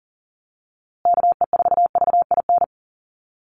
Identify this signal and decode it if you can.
Morse code, heard as a keyed tone, about 40 wpm, 700 Hz. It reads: KE4VIN
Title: USA Call Signs Repeated 3x 40wpm